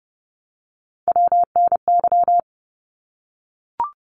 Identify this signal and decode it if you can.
Morse code, heard as a keyed tone, about 30 wpm, 700 Hz. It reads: WNY
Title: ARRL Sections 30wpm